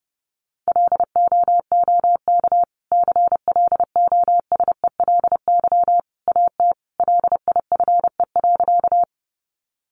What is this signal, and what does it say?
Morse code, heard as a keyed tone, about 30 wpm, 700 Hz. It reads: LOOK CLOSELY AT LIFE.
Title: Sentences from Top 1400 Words 30wpm